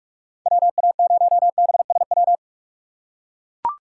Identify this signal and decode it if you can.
Morse code, heard as a keyed tone, about 42 wpm, 700 Hz. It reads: WA0BSW
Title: USA Call Signs - 2x Character Spacing - 45wpm